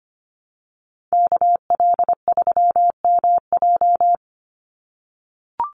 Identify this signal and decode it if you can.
Morse code, heard as a keyed tone, about 25 wpm, 700 Hz. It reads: KL3MJ
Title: USA Call Signs 25wpm